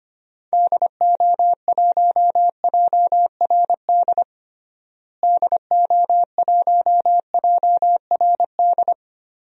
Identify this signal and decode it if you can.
Morse code, heard as a keyed tone, about 25 wpm, 700 Hz. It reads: DO1JRD DO1JRD
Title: DXCC-100 Call Signs - 6 Characters - Repeated 3x - 25wpm